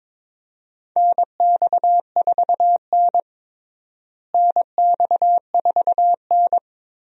Morse code, heard as a keyed tone, about 22 wpm, 700 Hz. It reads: NX4N NX4N
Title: USA Call Signs - 4 Characters - Repeated 3x - 22wpm